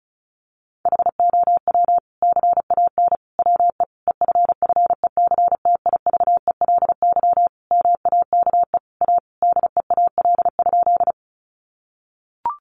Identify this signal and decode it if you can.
Morse code, heard as a keyed tone, about 34 wpm, 700 Hz. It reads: HOW CAN WE EFFECTIVELY MAKE A DEAL?
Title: Sentences from Top 1400 Words 35wpm